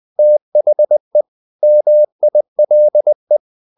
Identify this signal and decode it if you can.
Morse code, heard as a keyed tone, about 20 wpm, 600 Hz. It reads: THE MILE
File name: the old man and the sea part 2 20wpm